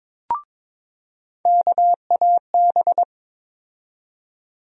Morse code, heard as a keyed tone, about 23 wpm, 700 Hz. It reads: KAB
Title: USA Call Sign Postfixes - 3 Letters 22wpm